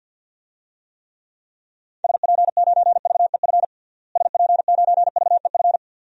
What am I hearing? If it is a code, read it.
Morse code, heard as a keyed tone, about 51 wpm, 700 Hz. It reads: SP9VEF SP9VEF
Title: DXCC-100 Call Signs - 6 Characters - Repeated 2x - 50wpm